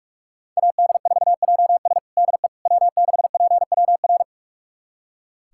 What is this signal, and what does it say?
Morse code, heard as a keyed tone, about 46 wpm, 700 Hz. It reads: AD4JS DE W6PWR
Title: USA Callsign DE Callsign - 5 Characters - 45wpm